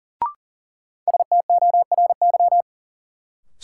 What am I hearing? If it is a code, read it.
Morse code, heard as a keyed tone, about 40 wpm, 700 Hz. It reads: STORY
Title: Sets of 3 Words - New Word Drill - 40wpm